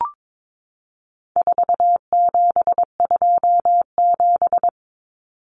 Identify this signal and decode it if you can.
Morse code, heard as a keyed tone, about 22 wpm, 700 Hz. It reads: 4727
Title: Numbers - 4-Digit 22wpm